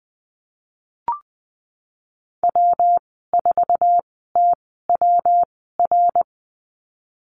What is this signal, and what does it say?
Morse code, heard as a keyed tone, about 20 wpm, spaced as for 15 wpm, 700 Hz. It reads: W4TWR
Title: 173 Lesson 28g - Number 9 - Call Signs 20wpm